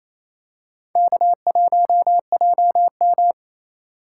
Morse code, heard as a keyed tone, about 28 wpm, 700 Hz. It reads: K1JM
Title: USA Call Signs - 4 Characters - 4x Character Spacing - 28wpm